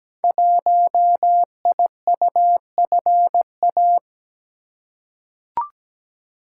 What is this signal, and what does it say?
Morse code, heard as a keyed tone, about 17 wpm, 700 Hz. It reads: 1IUFA
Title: Code Groups - 5 Letter:Numbers - 17wpm